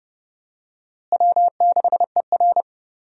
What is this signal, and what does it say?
Morse code, heard as a keyed tone, about 30 wpm, 700 Hz. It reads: W6ER
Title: USA Call Signs - 4 Characters - Repeated 2x - 30wpm